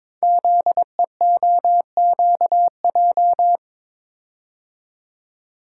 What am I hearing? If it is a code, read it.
Morse code, heard as a keyed tone, about 23 wpm, 700 Hz. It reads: ZEOQJ
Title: Code Groups - 5 Letters - 22wpm